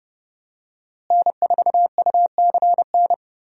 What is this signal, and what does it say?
Morse code, heard as a keyed tone, about 31 wpm, 700 Hz. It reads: N4UCN
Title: USA Call Signs - 5 Characters - Repeated 3x - 30wpm